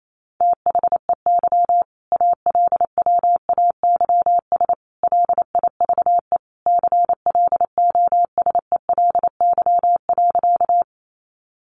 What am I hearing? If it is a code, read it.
Morse code, heard as a keyed tone, about 28 wpm, 700 Hz. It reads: THEY ALWAYS LIVE CLOSELY.
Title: Sentences from Top 1400 Words 28wpm